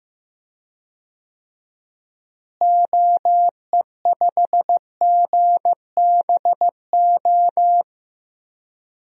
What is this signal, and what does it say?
Morse code, heard as a keyed tone, about 15 wpm, 700 Hz. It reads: OE5GBO
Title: DXCC-100 Call Signs - 6 Characters - Repeated 2x - 15wpm